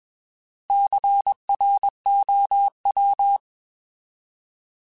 Morse code, heard as a keyed tone, about 21 wpm, 800 Hz. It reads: CROW